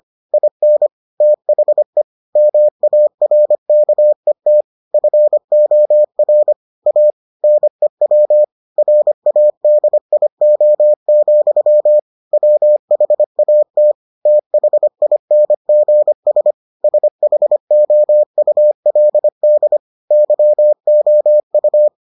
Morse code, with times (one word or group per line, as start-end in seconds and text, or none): 0.3-0.9 IN
1.2-2.0 THE
2.4-4.6 MARKET
4.9-6.5 FOR
6.9-7.1 A
7.4-8.4 NEW
8.8-12.0 RADIO,
12.3-13.9 WHAT
14.3-16.5 THINGS
16.8-19.8 SHOULD
20.1-21.9 YOU